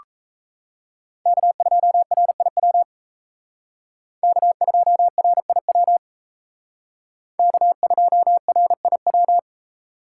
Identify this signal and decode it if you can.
Morse code, heard as a keyed tone, about 36 wpm, 700 Hz. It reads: K2RIW K2RIW K2RIW